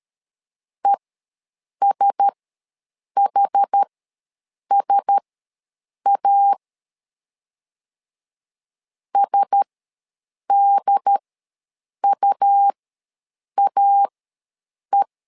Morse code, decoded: ESHSA SDUAE

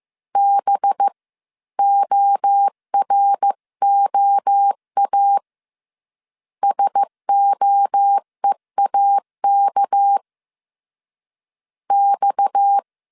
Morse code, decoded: B OROA SOEAK X

